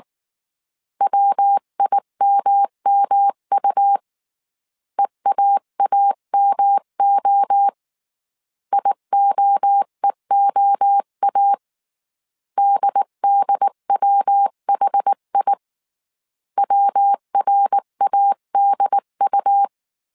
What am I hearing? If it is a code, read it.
Morse code, heard as a keyed tone, about 19 wpm, 800 Hz. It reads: WIMMU EAAMO IOEOA DDWHI WRADU
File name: CW / DET / 1 3 1500C S B 18wpm 1916